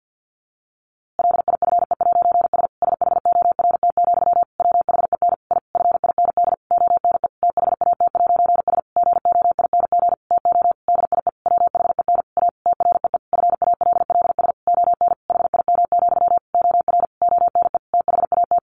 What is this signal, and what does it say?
Morse code, heard as a keyed tone, about 50 wpm, 700 Hz. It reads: LIFE'S SHORT, WHEN I FIND ONE THAT'S GOING TO DIE WHEN A TREE FALLS ON HIM, OR ONE THAT
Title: A Sound Of Thunder 50wpm